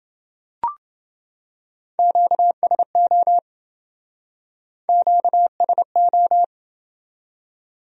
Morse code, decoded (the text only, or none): QSO QSO